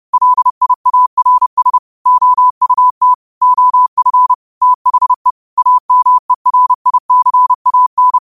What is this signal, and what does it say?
Morse code, heard as a keyed tone, about 30 wpm, 1000 Hz. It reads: LIARS OUT OF THE AMERICAN
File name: qotdfiles / 30wpm